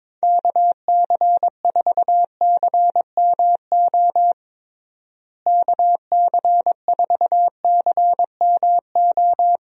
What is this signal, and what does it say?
Morse code, heard as a keyed tone, about 21 wpm, 700 Hz. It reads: KC4CMO KC4CMO
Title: USA Call Signs - 6 Characters - Repeated 3x - 22wpm